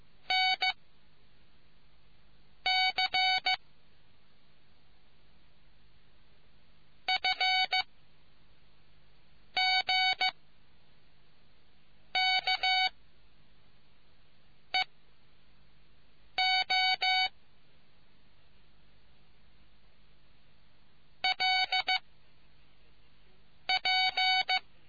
Morse code, heard as a keyed tone, about 15 wpm, 2250 Hz. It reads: N C F G K E O L P